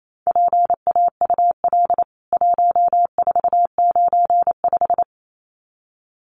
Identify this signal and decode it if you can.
Morse code, heard as a keyed tone, about 28 wpm, 700 Hz. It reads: PAUL 1495
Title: CWT Exchange 28wpm